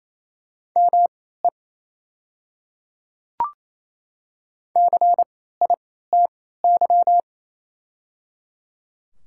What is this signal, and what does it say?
Morse code, heard as a keyed tone, about 28 wpm, spaced as for 16 wpm, 700 Hz. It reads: ME CITY
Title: 3x Character Spacing - Top 300 Words - 28wpm